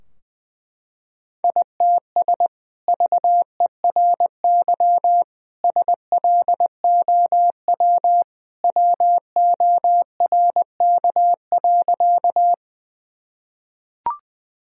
Morse code, heard as a keyed tone, about 20 wpm, 700 Hz. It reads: ITS VERY SLOW WORK.
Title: Sentences from Top 1000 Words - Review Part 2 - 20wpm